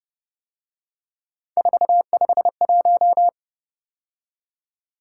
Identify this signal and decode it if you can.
Morse code, heard as a keyed tone, about 29 wpm, 700 Hz. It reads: 451